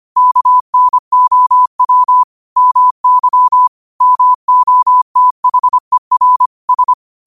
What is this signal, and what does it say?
Morse code, heard as a keyed tone, about 25 wpm, 1000 Hz. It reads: KNOW MY MOTHER S